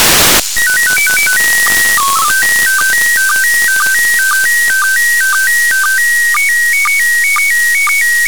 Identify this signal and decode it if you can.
SSTV (slow-scan television), PD 120.
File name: commit / satnogs 29412 short